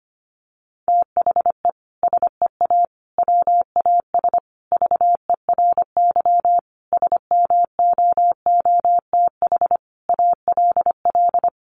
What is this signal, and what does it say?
Morse code, heard as a keyed tone, about 25 wpm, 700 Hz. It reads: THE SEA WAS VERY SMOOTH ALL
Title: Sentences from Top 1400 Words 25wpm